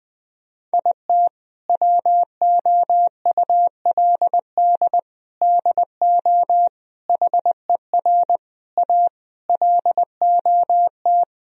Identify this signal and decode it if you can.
Morse code, heard as a keyed tone, about 20 wpm, 700 Hz. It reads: IT WOULD DO HER A LOT
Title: Sentences from Top 1000 Words - Review Part 3 - 20wpm